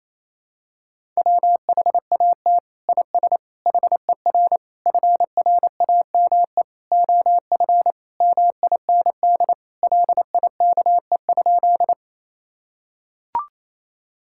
Morse code, heard as a keyed tone, about 27 wpm, 700 Hz. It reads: WHAT IS HER FRAME OF MIND LIKE?
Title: Sentences from Top 1000 Words - Review Part 6 - 28wpm